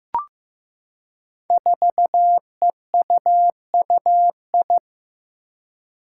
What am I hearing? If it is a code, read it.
Morse code, heard as a keyed tone, about 15 wpm, 700 Hz. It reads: 4EUUI
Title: Five Challenging Letter-Number Combinations 15wpm